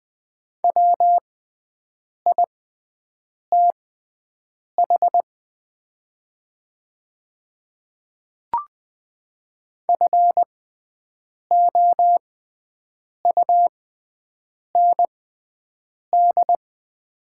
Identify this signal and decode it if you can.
Morse code, heard as a keyed tone, about 20 wpm, spaced as for 7 wpm, 700 Hz. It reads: WITH FOUND
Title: 6x Character Spacing - Top 300 Words - 20wpm